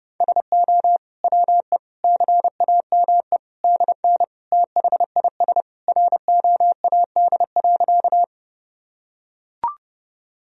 Morse code, heard as a keyed tone, about 30 wpm, 700 Hz. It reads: SO WE CAME DN THIS ROAD.
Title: Sentences from Top 400 Words and Abbreviations 30wpm